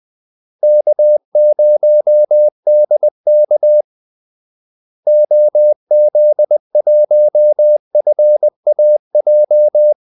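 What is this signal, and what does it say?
Morse code, heard as a keyed tone, about 20 wpm, 600 Hz. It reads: K0DK OZ1FAJ
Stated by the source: Morse call-sign practice at 20 wpm.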